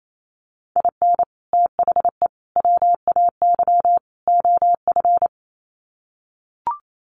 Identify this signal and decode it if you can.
Morse code, heard as a keyed tone, about 28 wpm, 700 Hz. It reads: IN THE WAY OF
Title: Sets of 4 Words 28wpm